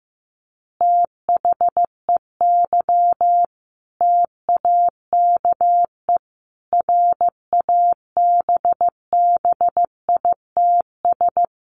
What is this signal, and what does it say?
Morse code, heard as a keyed tone, about 15 wpm, 700 Hz. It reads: THEY TAKE RABBITS
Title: Easy Phrases 15wpm